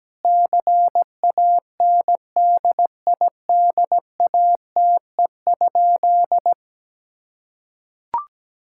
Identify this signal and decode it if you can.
Morse code, heard as a keyed tone, about 17 wpm, 700 Hz. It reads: CANDIDATE?
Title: Sentences from Top 1600 Words 17wpm